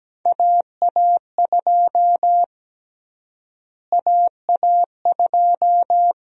Morse code, heard as a keyed tone, about 17 wpm, 700 Hz. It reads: AA2 AA2
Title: USA Call Sign Prefixes - Repeated 2x - 17wpm